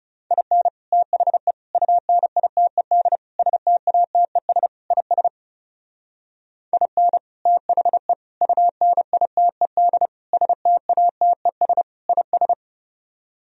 Morse code, heard as a keyed tone, about 36 wpm, 700 Hz. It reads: IN THE UNITED STATES IS IN THE UNITED STATES IS
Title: Speed-Racing - Sets of 5 Words 30wpm